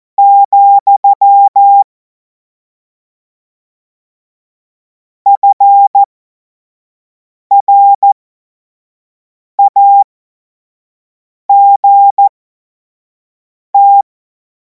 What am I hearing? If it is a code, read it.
Morse code, heard as a keyed tone, about 14 wpm, spaced as for 5 wpm, 800 Hz. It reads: , FRAGT
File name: Morsekurs / Witze / Witz 24. März 2026 25 F70